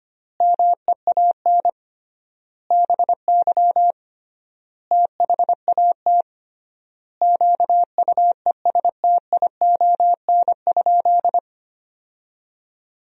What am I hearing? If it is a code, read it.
Morse code, heard as a keyed tone, about 25 wpm, 700 Hz. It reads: MEAN BY THAT QUESTION?
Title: 3x Word-Spacing - Sentences from Top 400 Words - 25wpm